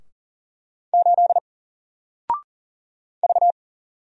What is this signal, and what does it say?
Morse code, heard as a keyed tone, about 40 wpm, 700 Hz. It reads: 8 V
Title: Single Character (Letter:Number:Punctuation:Prosign) Alt 40wpm